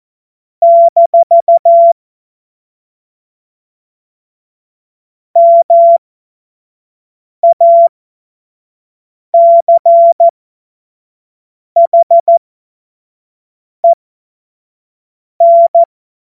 Morse code, heard as a keyed tone, about 14 wpm, spaced as for 5 wpm, 680 Hz. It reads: - MACHEN